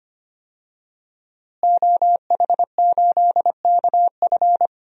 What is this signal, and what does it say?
Morse code, heard as a keyed tone, about 25 wpm, 700 Hz. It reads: OH8KF
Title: DXCC-100 Call Signs - 5 Characters - Repeated 3x - 25wpm